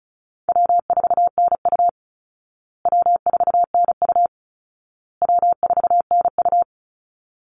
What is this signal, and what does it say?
Morse code, heard as a keyed tone, about 34 wpm, 700 Hz. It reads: W4NU W4NU W4NU
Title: Easy Call Signs Repeated 3x - 35wpm